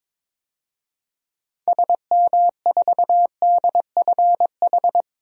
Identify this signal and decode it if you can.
Morse code, heard as a keyed tone, about 22 wpm, 700 Hz. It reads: SM4DFH
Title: DXCC-100 Call Signs - 6 Characters - 22wpm